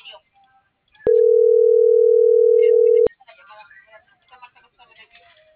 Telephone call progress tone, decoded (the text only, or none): ringback tone